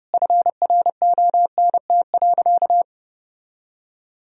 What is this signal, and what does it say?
Morse code, heard as a keyed tone, about 32 wpm, 700 Hz. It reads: FRONT.
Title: Sentences from Top 1000 Words - Review Part 2 - 30wpm